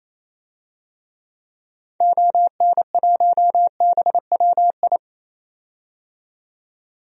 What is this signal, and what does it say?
Morse code, heard as a keyed tone, about 28 wpm, 700 Hz. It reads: ON1BWI
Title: DXCC-100 Call Signs - 6 Characters - 28wpm